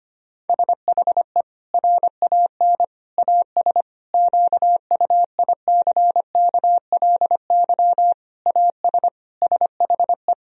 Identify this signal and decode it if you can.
Morse code, heard as a keyed tone, about 25 wpm, 700 Hz. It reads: SHE RAN AS QUICKLY AS SHE
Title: Sentences from Top 1000 Words - Review Part 4 - 25wpm